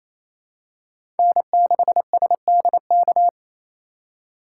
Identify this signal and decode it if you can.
Morse code, heard as a keyed tone, about 28 wpm, 700 Hz. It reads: N6SDK
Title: USA Call Signs - 3x Character Spacing - 28wpm